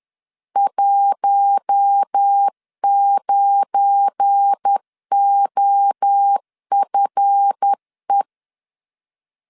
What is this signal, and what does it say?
Morse code, heard as a keyed tone, about 10 wpm, 800 Hz. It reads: 19OFE
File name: LEZIONI / 7 GF9Z / 7 GF9 10wpm 2747